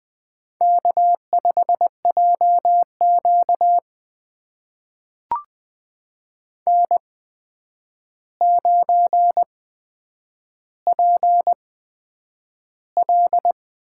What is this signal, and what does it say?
Morse code, heard as a keyed tone, about 20 wpm, spaced as for 9 wpm, 700 Hz. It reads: K5JQ N9PL
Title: USA Call Signs - 4 Characters - 8x Character Spacing - 20wpm